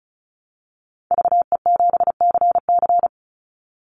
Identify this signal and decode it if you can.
Morse code, heard as a keyed tone, about 34 wpm, 700 Hz. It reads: VE7CC
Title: Call Signs Repeated 2x - 35wpm